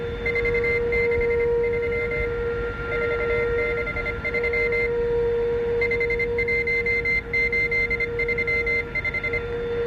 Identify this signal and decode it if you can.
Morse code, heard as a keyed tone, about 25 wpm, 2100 Hz. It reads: 463 463 51835